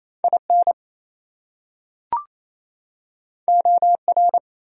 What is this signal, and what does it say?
Morse code, heard as a keyed tone, about 28 wpm, 700 Hz. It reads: IN OR
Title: Top 100 Words 28wpm